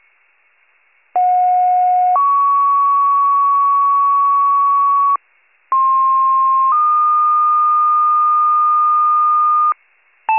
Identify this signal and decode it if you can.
Two-tone page, heard as a one-second tone, then a three-second tone, about 730 then 1120 Hz, about 1060 then 1220 Hz. It is heard from 1.2 to 9.7 s.